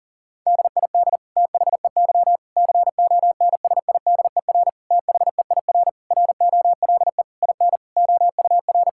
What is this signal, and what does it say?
Morse code, heard as a keyed tone, about 40 wpm, 700 Hz. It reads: DID THEY CONSIDER THEIR ROLE IN OUR
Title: Sentences from Top 1000 Words - Review Part 6 - 40wpm